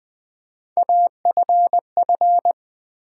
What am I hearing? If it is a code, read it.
Morse code, heard as a keyed tone, about 20 wpm, 700 Hz. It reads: AFF
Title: Build A Word - 3 Letters At A Time - 6-Letter Words - 20wpm